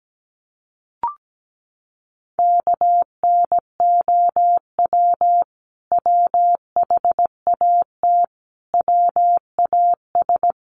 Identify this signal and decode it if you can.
Morse code, heard as a keyed tone, about 17 wpm, 700 Hz. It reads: KNOW WHAT WAS
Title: Sets of 5 Words Encore2 17wpm